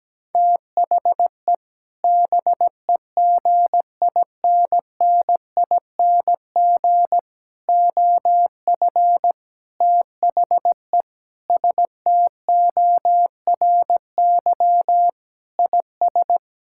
Morse code, heard as a keyed tone, about 17 wpm, 700 Hz. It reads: THE BEGINNING OF THE STORY IS
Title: Sentences from Top 1000 Words - Review Part 6 - 17wpm